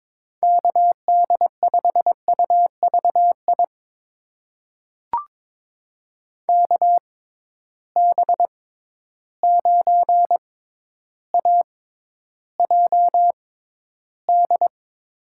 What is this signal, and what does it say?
Morse code, heard as a keyed tone, about 22 wpm, spaced as for 11 wpm, 700 Hz. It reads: KD5UVI KB9AJD
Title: USA Call Signs - 6 Characters - 6x Character Spacing - 22wpm